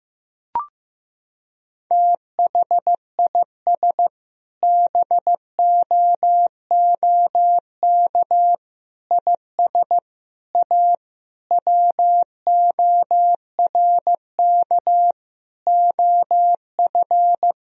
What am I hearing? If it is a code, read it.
Morse code, heard as a keyed tone, about 15 wpm, 700 Hz. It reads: THIS BOOK IS A WORK OF
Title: Sentences from Top 1000 Words - Review Part 3 - 15wpm